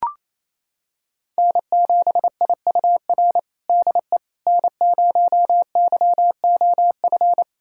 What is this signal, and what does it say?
Morse code, heard as a keyed tone, about 28 wpm, 700 Hz. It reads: N7IUR DE N0YOF
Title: USA Callsign DE Callsign - 5 Characters - 28wpm